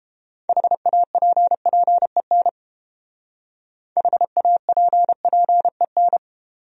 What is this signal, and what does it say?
Morse code, heard as a keyed tone, about 30 wpm, 700 Hz. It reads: HAPPEN HAPPEN